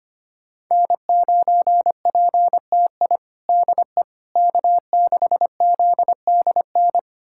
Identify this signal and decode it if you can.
Morse code, heard as a keyed tone, about 25 wpm, 700 Hz. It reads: N9PTI DE K6ZDN
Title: USA Callsign DE Callsign - 5 Characters - 25wpm